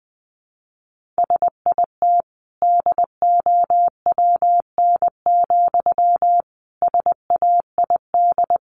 Morse code, heard as a keyed tone, about 20 wpm, 700 Hz. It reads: SIT DOWN, SAID
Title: Sentences from Top 500 Words 20wpm